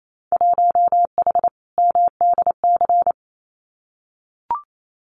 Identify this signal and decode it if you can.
Morse code, heard as a keyed tone, about 28 wpm, 700 Hz. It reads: 1H MDC
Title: Winter Field Day Exchanges 28wpm